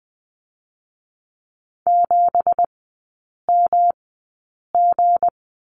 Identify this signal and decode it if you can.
Morse code, heard as a keyed tone, about 20 wpm, spaced as for 8 wpm, 700 Hz. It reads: 7MG
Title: 166 Lesson 27e - Number 7 - Three Characters 20wpm